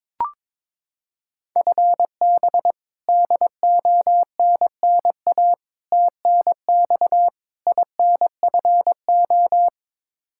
Morse code, heard as a keyed tone, about 22 wpm, 700 Hz. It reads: FB DONNA TNX INFO
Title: Practice QSO - Exchange 3 - More Info - 22wpm